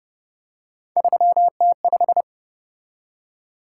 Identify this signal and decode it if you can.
Morse code, heard as a keyed tone, about 30 wpm, 700 Hz. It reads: 3T5